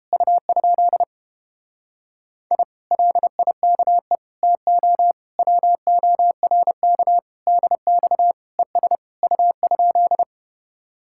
Morse code, heard as a keyed tone, about 31 wpm, 700 Hz. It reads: U? I LIKE TO WORK DX ES U?